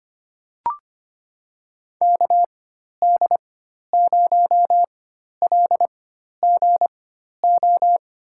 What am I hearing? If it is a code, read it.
Morse code, heard as a keyed tone, about 25 wpm, spaced as for 12 wpm, 700 Hz. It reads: KD0LGO
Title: USA Call Signs - 4x Character Spacing - 25wpm